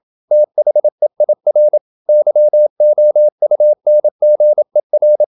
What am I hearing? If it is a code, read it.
Morse code, heard as a keyed tone, about 27 wpm, 600 Hz. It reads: THEIR YOUNGER
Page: Never 26wpm | CW med Gnister